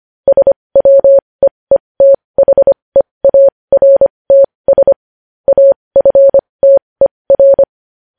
Morse code, heard as a keyed tone, about 25 wpm, spaced as for 20 wpm, 550 Hz. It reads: SWEETHEARTS AFTER
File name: podcast / News headlines-1x-25x20